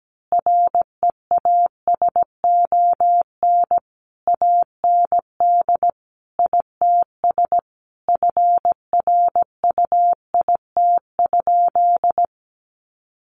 Morse code, heard as a keyed tone, about 17 wpm, 700 Hz. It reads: REASON AND ITS FRUIT?
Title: Sentences from Top 1500 Words 17wpm